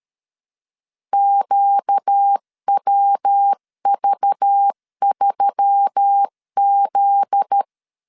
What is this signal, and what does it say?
Morse code, heard as a keyed tone, about 13 wpm, 800 Hz. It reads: QWV3Z
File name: LEZIONI / 7 GF9Z / 7 GF9Z 12wpm 2147